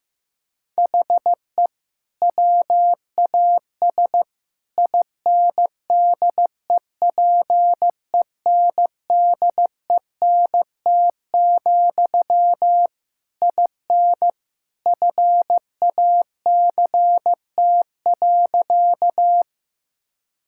Morse code, heard as a keyed tone, about 16 wpm, 700 Hz. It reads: HE WAS INDEPENDENT, IN FACT.
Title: Sentences from Top 1000 Words - Review Part 6 - 15wpm